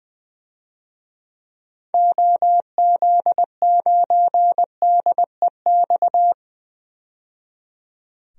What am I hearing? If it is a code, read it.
Morse code, heard as a keyed tone, about 20 wpm, 700 Hz. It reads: OZ9DEX
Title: DXCC-100 Call Signs - 6 Characters - 20wpm